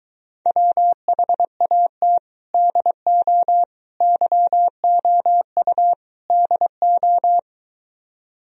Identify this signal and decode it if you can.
Morse code, heard as a keyed tone, about 22 wpm, 700 Hz. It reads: WHAT DO YOU DO